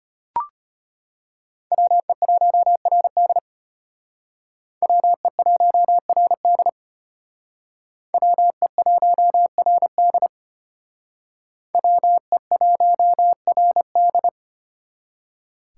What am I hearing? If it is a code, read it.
Morse code, heard as a keyed tone, about 31 wpm, 700 Hz. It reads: WE1RD WE1RD WE1RD WE1RD
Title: Speed Racing - USA 2x2 Call Signs 25wpm